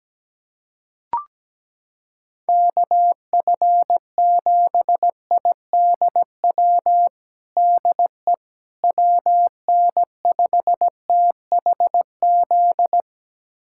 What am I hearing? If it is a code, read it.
Morse code, heard as a keyed tone, about 17 wpm, 700 Hz. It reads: KF7IDW DE WN5THZ
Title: USA Callsign DE Callsign - 6 Characters - 17wpm